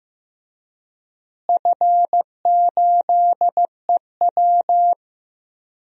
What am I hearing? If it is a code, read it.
Morse code, heard as a keyed tone, about 15 wpm, 700 Hz. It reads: F8EW
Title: DXCC-100 Call Signs - 4 Characters - Repeated 3x - 15wpm